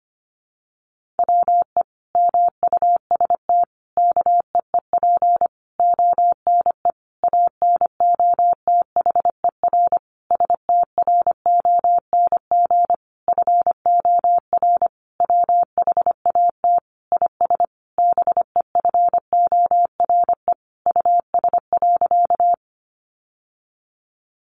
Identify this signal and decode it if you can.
Morse code, heard as a keyed tone, about 25 wpm, 700 Hz. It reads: WE MUST KEEP ONE ANOTHER STRONG FOR WHAT IS BEFORE US.
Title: Sentences from Top 600 Words 25wpm